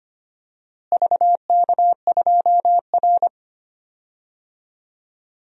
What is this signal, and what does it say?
Morse code, heard as a keyed tone, about 25 wpm, 700 Hz. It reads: VK2R
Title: DXCC-100 Call Signs - 4 Characters - Repeated 3x - 25wpm